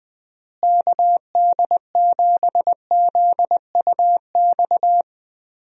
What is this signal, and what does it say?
Morse code, heard as a keyed tone, about 20 wpm, 700 Hz. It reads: KD7ZUX
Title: USA Call Signs - 6 Characters - Repeated 2x - 20wpm